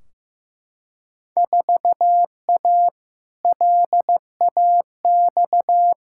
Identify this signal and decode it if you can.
Morse code, heard as a keyed tone, about 15 wpm, 700 Hz. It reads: 4A LAX
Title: Field Day Exchanges - 15wpm